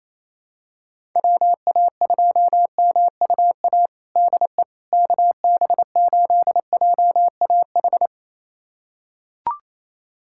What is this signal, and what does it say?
Morse code, heard as a keyed tone, about 28 wpm, 700 Hz. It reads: WA2MUA DE KB8JAH
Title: USA Callsign DE Callsign 28wpm